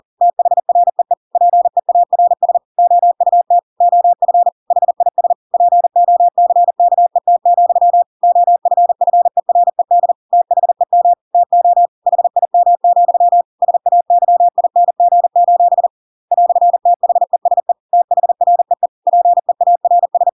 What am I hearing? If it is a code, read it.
Morse code, heard as a keyed tone, about 40 wpm, 700 Hz. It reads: THREE PEARS OUT OF HIS POCKET, OFFERED THEM TO HIM, SAYING: "THESE THREE PEARS
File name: Pinocchio / Pinocchio ch7 40 40 ik1hge